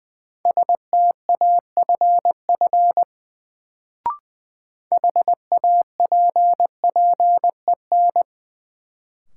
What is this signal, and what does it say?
Morse code, heard as a keyed tone, about 20 wpm, 700 Hz. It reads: STAFF HAPPEN
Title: Sets of 5 Words Encore - New Word Drill - 20wpm